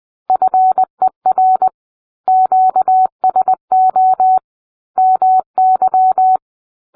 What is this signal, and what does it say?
Morse code, heard as a keyed tone, about 20 wpm, 750 Hz. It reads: FER QSO MY